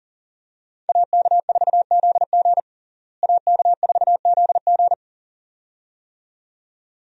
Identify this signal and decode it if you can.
Morse code, heard as a keyed tone, about 40 wpm, 700 Hz. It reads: AK4ZG AK4ZG
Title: USA Call Signs - 5 Characters - Repeated 2x - 40wpm